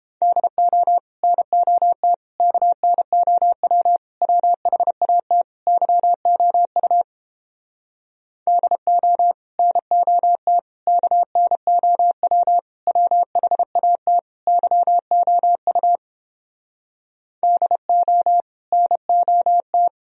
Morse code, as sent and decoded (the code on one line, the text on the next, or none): -.. --- / -. --- - / -.- -. --- .-- / .-- .... .- - / -.-- --- ..- / -.. --- / -. --- - / -.- -. --- .-- / .-- .... .- - / -.-- --- ..- / -.. --- / -. --- -
DO NOT KNOW WHAT YOU DO NOT KNOW WHAT YOU DO NOT